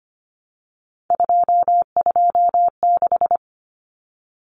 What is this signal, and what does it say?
Morse code, heard as a keyed tone, about 25 wpm, 700 Hz. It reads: 226
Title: Numbers - 3-Digit 25wpm